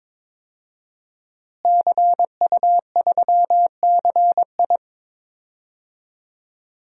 Morse code, heard as a keyed tone, about 22 wpm, 700 Hz. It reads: CU3CI
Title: DXCC-100 Call Signs - 5 Characters - Repeated 2x - 22wpm